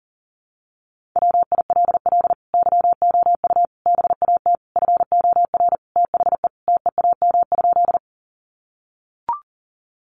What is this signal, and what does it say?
Morse code, heard as a keyed tone, about 40 wpm, 700 Hz. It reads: WILL YOU BAT FOR THE TEAM?
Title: Sentences from Top 1500 Words 40wpm